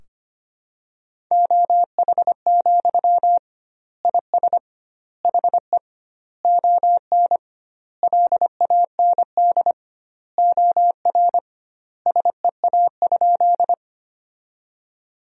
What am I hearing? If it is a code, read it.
Morse code, heard as a keyed tone, about 25 wpm, 700 Hz. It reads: OH, IS HE ON LAND OR SEA?
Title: 2x Word-Spacing - Sentences from Top 500 Words - 25wpm